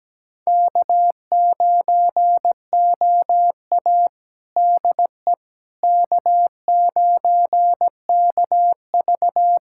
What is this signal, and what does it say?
Morse code, heard as a keyed tone, about 17 wpm, 700 Hz. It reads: K9OA DE K9KV
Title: USA Callsign DE Callsign - 4 Characters - 17wpm